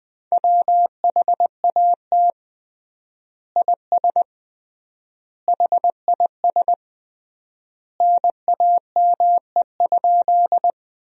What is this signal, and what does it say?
Morse code, heard as a keyed tone, about 21 wpm, 700 Hz. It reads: WHAT IS HIS NAME?
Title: 3x Word-Spacing - Sentences from Top 200 Words - 20wpm